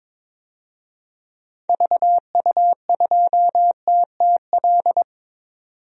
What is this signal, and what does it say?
Morse code, heard as a keyed tone, about 22 wpm, 700 Hz. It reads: VU2TTL
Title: DXCC-100 Call Signs - 6 Characters - 22wpm